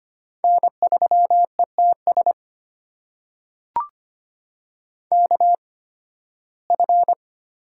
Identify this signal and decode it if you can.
Morse code, heard as a keyed tone, about 25 wpm, spaced as for 16 wpm, 700 Hz. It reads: N3ETS KF
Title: USA Call Signs - 5 Characters - 8x Character Spacing - 25wpm